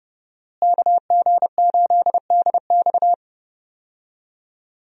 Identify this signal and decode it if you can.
Morse code, heard as a keyed tone, about 30 wpm, 700 Hz. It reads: KG8DX
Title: USA Call Signs - 5 Characters - Repeated 2x - 30wpm